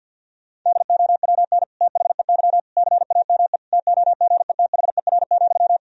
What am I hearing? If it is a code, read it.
Morse code, heard as a keyed tone, about 51 wpm, 700 Hz. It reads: DOWN THEY CAME TOGETHER,
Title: Sentences from Top 1000 Words - Review Part 2 - 50wpm